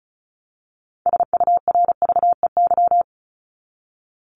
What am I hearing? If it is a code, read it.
Morse code, heard as a keyed tone, about 34 wpm, 700 Hz. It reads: SURVEY
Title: Top 1501-1750 Words 35wpm